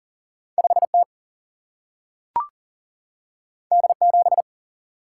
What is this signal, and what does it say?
Morse code, heard as a keyed tone, about 39 wpm, 700 Hz. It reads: 5 T D 7
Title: Single-Letter Single-Number 40wpm